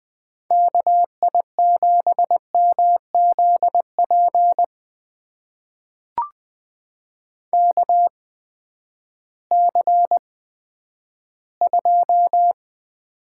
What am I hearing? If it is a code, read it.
Morse code, heard as a keyed tone, about 20 wpm, spaced as for 11 wpm, 700 Hz. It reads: KI7MZP KC2
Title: USA Call Signs - 6 Characters - 8x Character Spacing - 20wpm